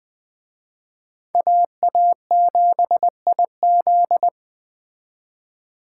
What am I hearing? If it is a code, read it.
Morse code, heard as a keyed tone, about 20 wpm, 700 Hz. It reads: AA7IZ